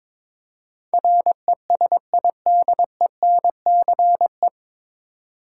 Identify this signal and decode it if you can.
Morse code, heard as a keyed tone, about 21 wpm, 700 Hz. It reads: RESIDENCE
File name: 9 Letter Words - First 3 Letters Given - 22wpm